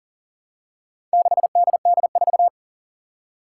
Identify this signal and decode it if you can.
Morse code, heard as a keyed tone, about 40 wpm, 700 Hz. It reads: 6DD4
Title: Four Challenging Letter-Number Combinations 40wpm